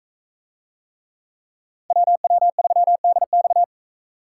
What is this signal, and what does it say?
Morse code, heard as a keyed tone, about 42 wpm, 700 Hz. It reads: WW3DX